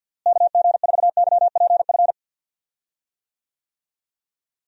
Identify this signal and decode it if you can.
Morse code, heard as a keyed tone, about 50 wpm, 700 Hz. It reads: KG4YPF
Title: USA Call Signs - 6 Characters - Repeated 3x - 50wpm